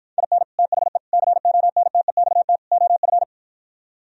Morse code, heard as a keyed tone, about 53 wpm, 700 Hz. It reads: IN THE CONTEXT OF